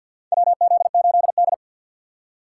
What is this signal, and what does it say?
Morse code, heard as a keyed tone, about 49 wpm, 700 Hz. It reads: WZ8D
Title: USA Call Signs - 4 Characters - 6x Character Spacing - 50wpm